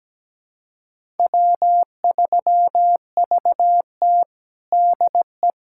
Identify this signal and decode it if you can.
Morse code, heard as a keyed tone, about 17 wpm, 700 Hz. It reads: W3VT DE
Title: USA Callsign DE Callsign - 4 Characters - 17wpm